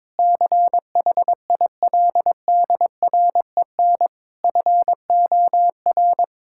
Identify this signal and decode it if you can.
Morse code, heard as a keyed tone, about 22 wpm, 700 Hz. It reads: CHILDREN FOR
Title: Sentences from Top 1000 Words - Review Part 6 - 22wpm